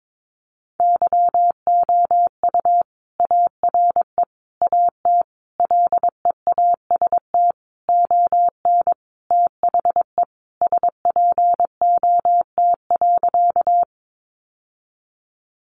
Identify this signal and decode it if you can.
Morse code, heard as a keyed tone, about 22 wpm, 700 Hz. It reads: YOU ARE AT LEAST ON THE SPOT.
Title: Sentences from Top 1100 Words 22wpm